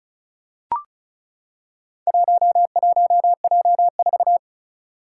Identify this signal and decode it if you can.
Morse code, heard as a keyed tone, about 36 wpm, 700 Hz. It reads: 11J4
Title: Four Challenging Letter-Number Combinations 35wpm